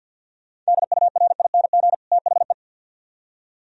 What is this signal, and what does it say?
Morse code, heard as a keyed tone, about 62 wpm, 700 Hz. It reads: DURING THE
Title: Speed-Racing - Sets of 2 Words 50wpm